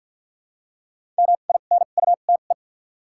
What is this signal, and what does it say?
Morse code, heard as a keyed tone, about 53 wpm, 700 Hz. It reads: M I N U T E
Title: 2x Character Spacing - Top 500 Words - 50wpm